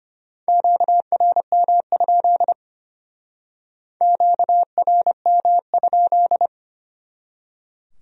Morse code, heard as a keyed tone, about 33 wpm, 700 Hz. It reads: QRM? QRM?